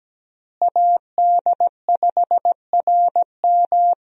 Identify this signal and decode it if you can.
Morse code, heard as a keyed tone, about 17 wpm, 700 Hz. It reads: AD5RM
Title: USA Call Signs - 5 Characters - Repeated 2x - 17wpm